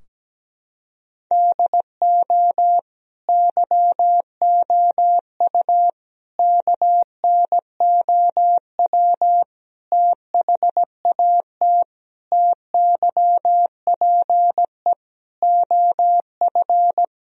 Morse code, heard as a keyed tone, about 17 wpm, 700 Hz. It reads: DO YOU KNOW THAT TYPE OF
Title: Sentences from Top 1000 Words - Review Part 3 - 17wpm